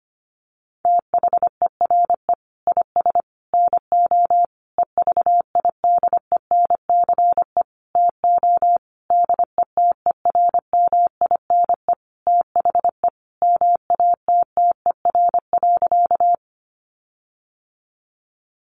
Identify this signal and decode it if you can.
Morse code, heard as a keyed tone, about 25 wpm, 700 Hz. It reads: THERE IS NO EVIDENCE TO DETERMINE THE MATTER.
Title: Sentences from Top 1300 Words 25wpm